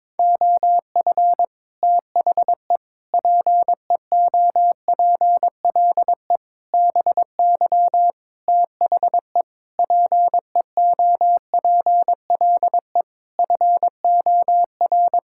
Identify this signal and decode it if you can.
Morse code, heard as a keyed tone, about 22 wpm, 700 Hz. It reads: OF THE PEOPLE BY THE PEOPLE FOR
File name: practice / 22wpm / Sets of 8 Words - 22wpm